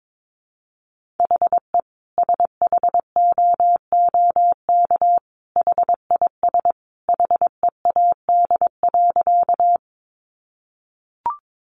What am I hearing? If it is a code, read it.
Morse code, heard as a keyed tone, about 22 wpm, 700 Hz. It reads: HE SHOOK HIS HEAD.
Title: A Sound Of Thunder 22wpm